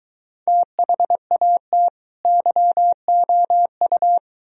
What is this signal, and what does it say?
Morse code, heard as a keyed tone, about 22 wpm, 700 Hz. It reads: THAT YOU